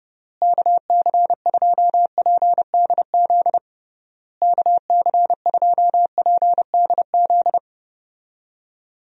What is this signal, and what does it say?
Morse code, heard as a keyed tone, about 30 wpm, 700 Hz. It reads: KC2PDZ KC2PDZ
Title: USA Call Signs - 6 Characters - Repeated 3x - 30wpm